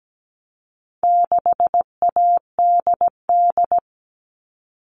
Morse code, heard as a keyed tone, about 17 wpm, 700 Hz. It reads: 6ADD
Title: All Cut Numbers - 4-Digit 17wpm